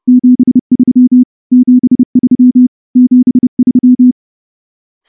Morse code, heard as a keyed tone, about 30 wpm, 250 Hz. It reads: 73 73 73
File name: morse-training-30WPM